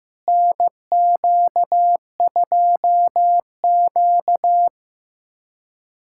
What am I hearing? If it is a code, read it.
Morse code, heard as a keyed tone, about 15 wpm, 700 Hz. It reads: NQ2Q
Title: USA Callsign DE Callsign - 4 Characters - 15wpm